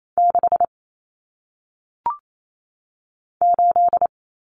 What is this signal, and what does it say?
Morse code, heard as a keyed tone, about 28 wpm, 700 Hz. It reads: 6 8
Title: Numbers - 1-Digit 28wpm